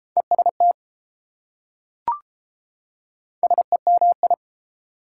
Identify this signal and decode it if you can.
Morse code, heard as a keyed tone, about 32 wpm, 690 Hz. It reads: EST SEMI